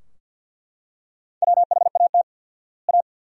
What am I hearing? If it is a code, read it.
Morse code, heard as a keyed tone, about 50 wpm, 700 Hz. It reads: WHAT A
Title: 4x Word-Spacing - Sentences from Top 200 Words - 50wpm